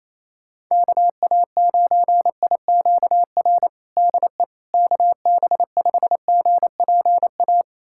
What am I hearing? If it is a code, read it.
Morse code, heard as a keyed tone, about 28 wpm, 700 Hz. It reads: KA9IQR DE KB5GPA
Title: USA Callsign DE Callsign - 6 Characters - 28wpm